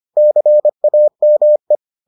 Morse code, heard as a keyed tone, about 25 wpm, 600 Hz. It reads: CAME